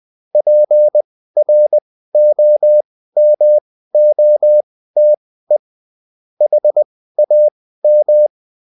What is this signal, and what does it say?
Morse code, heard as a keyed tone, about 20 wpm, spaced as for 14 wpm, 600 Hz. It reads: PROMOTE HAM